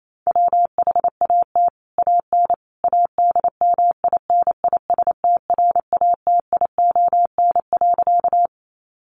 Morse code, heard as a keyed tone, about 28 wpm, 700 Hz. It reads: WHAT AN ADMINISTRATION.
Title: Sentences from Top 1300 Words 28wpm